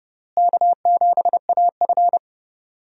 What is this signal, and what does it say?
Morse code, heard as a keyed tone, about 32 wpm, 700 Hz. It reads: K7AF
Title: USA Call Signs - 4 Characters - Repeated 3x - 30wpm